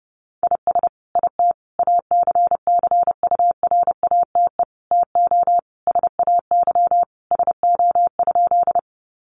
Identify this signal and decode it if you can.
Morse code, heard as a keyed tone, about 30 wpm, 700 Hz. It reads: IS IT ACCURATE TO SAY SO?
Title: Sentences from Top 1200 Words 30wpm